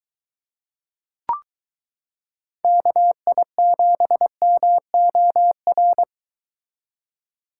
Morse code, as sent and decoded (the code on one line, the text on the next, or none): -.- .. --... -- --- .-.
KI7MOR